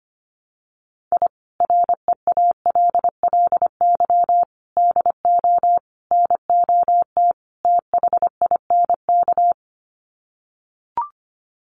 Morse code, heard as a keyed tone, about 25 wpm, 700 Hz. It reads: I REALLY DO NOT THINK
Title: Sets of 5 Words Encore2 25wpm